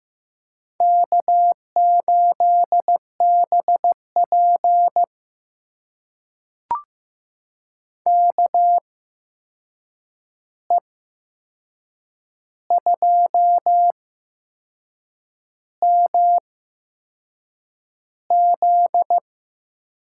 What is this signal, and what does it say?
Morse code, heard as a keyed tone, about 15 wpm, spaced as for 6 wpm, 700 Hz. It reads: K8BP KE2MZ
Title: USA Call Signs - 8x Character Spacing - 15wpm